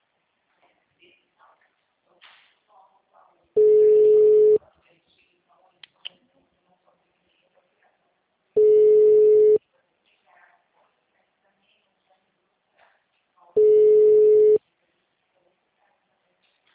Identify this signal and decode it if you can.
Ringback tone (Europe, 425 Hz), with speech.
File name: out- -2005- - - .11355